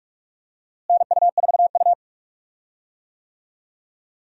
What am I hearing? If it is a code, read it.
Morse code, heard as a keyed tone, about 46 wpm, 700 Hz. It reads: NU4U
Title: USA Call Signs - 4 Characters - Repeated 3x - 45wpm